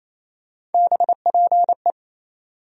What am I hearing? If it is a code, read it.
Morse code, heard as a keyed tone, about 28 wpm, 700 Hz. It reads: BPE
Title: USA Call Sign Postfixes - 3 Letters 28wpm